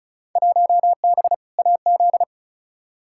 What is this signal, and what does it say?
Morse code, heard as a keyed tone, about 34 wpm, 700 Hz. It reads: 1B AZ
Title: Field Day Exchanges Encore - 35wpm